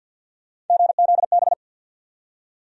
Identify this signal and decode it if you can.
Morse code, heard as a keyed tone, about 49 wpm, 700 Hz. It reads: GZB
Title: USA Call Sign Postfixes - 3 Letters 50wpm